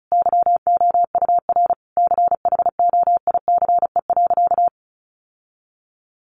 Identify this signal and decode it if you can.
Morse code, heard as a keyed tone, about 34 wpm, 700 Hz. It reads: YOUR CHOICE.
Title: Sentences from Top 900 Words 35wpm